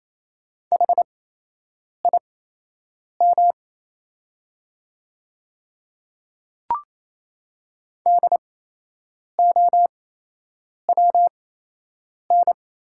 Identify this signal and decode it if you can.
Morse code, heard as a keyed tone, about 28 wpm, spaced as for 8 wpm, 700 Hz. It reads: HIM DOWN
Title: 8x Character Spacing - Top 100 Words - 28wpm